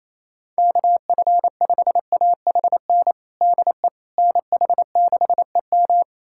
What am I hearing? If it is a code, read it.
Morse code, heard as a keyed tone, about 27 wpm, 700 Hz. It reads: KF5AHN DE NH6EM
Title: USA Callsign DE Callsign 28wpm